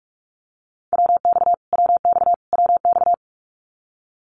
Morse code, heard as a keyed tone, about 44 wpm, 700 Hz. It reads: WX WX WX
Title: Abbreviations Repeated 3x - 45wpm